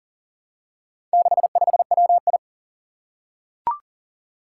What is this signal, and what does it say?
Morse code, heard as a keyed tone, about 40 wpm, 700 Hz. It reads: 65WI
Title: Four Challenging Letter-Number Combinations 40wpm